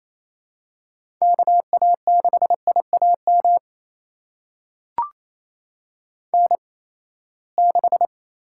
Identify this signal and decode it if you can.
Morse code, heard as a keyed tone, about 28 wpm, spaced as for 19 wpm, 700 Hz. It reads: KA6IAM N6
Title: USA Call Signs - 8x Character Spacing - 28wpm